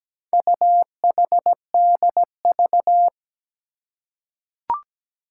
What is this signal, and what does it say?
Morse code, heard as a keyed tone, about 17 wpm, 700 Hz. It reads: UHDV
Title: Four Challenging Letter-Number Combinations 17wpm